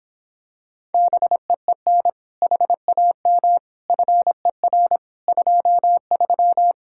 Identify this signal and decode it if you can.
Morse code, heard as a keyed tone, about 26 wpm, 700 Hz. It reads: BEEN HAM FER 23
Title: Speed-Racing Common QSO Phrases Encore - 22wpm